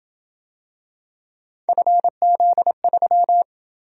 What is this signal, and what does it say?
Morse code, heard as a keyed tone, about 26 wpm, 700 Hz. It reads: FZ3